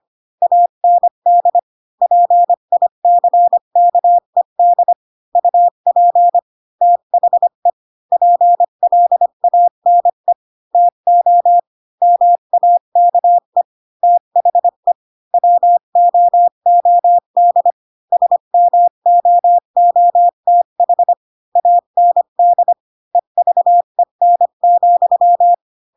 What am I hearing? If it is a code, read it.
Morse code, heard as a keyed tone, about 25 wpm, 700 Hz. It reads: AND PICKED UP THE PLANE TO MAKE THE WOOD SMOOTH AND EVEN,
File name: Pinocchio / Pinocchio ch1 20 25 ik1hge